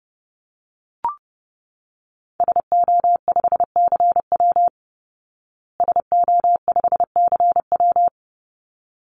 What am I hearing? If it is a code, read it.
Morse code, heard as a keyed tone, about 30 wpm, 700 Hz. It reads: SO5CW SO5CW
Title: Call Signs Repeated 3x - 30wpm